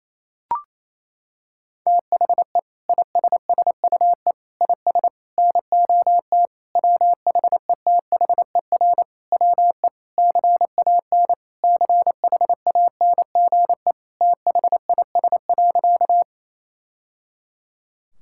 Morse code, heard as a keyed tone, about 28 wpm, 700 Hz. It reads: THE ISSUE IS NOT WHETHER WE CAN CHANGE THIS.
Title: Sentences from Top 1000 Words - Review Part 6 - 28wpm